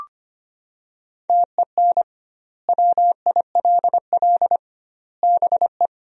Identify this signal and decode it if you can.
Morse code, heard as a keyed tone, about 25 wpm, 700 Hz. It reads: TEN WILL BE
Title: 2x Word-Spacing - Sentences from Top 400 Words - 25wpm